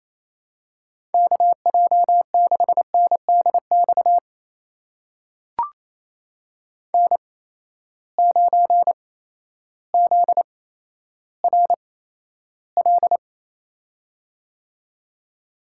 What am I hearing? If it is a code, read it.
Morse code, heard as a keyed tone, about 28 wpm, spaced as for 13 wpm, 700 Hz. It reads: KJ6NDX N9ZRL
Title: USA Call Signs - 8x Character Spacing - 28wpm